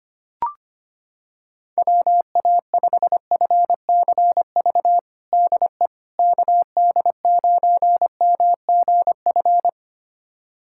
Morse code, heard as a keyed tone, about 25 wpm, 700 Hz. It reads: WA5FCV DE KD9MGF
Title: USA Callsign DE Callsign - 6 Characters - 25wpm